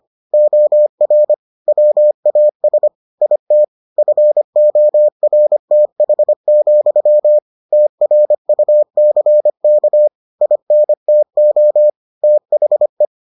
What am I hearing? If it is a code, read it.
Morse code, heard as a keyed tone, about 25 wpm, 600 Hz. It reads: OR WAS IT FORTH, TRUCK INTO THE